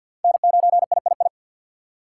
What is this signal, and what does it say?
Morse code, heard as a keyed tone, about 47 wpm, 700 Hz. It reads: N9III
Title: USA Call Signs - 5 Characters - 6x Character Spacing - 50wpm